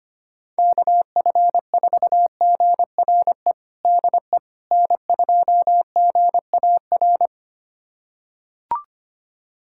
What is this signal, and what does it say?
Morse code, heard as a keyed tone, about 25 wpm, 700 Hz. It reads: KF4GRE DE N2GAR
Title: USA Callsign DE Callsign 25wpm